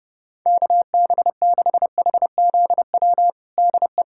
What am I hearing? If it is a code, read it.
Morse code, heard as a keyed tone, about 30 wpm, 700 Hz. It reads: KB6HZW DE
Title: USA Callsign DE Callsign - 6 Characters - 30wpm